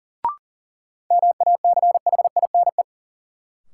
Morse code, heard as a keyed tone, about 40 wpm, 700 Hz. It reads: MACHINE
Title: Sentences from Top 1000 Words - Review Part 2 - New Word Drill - 40wpm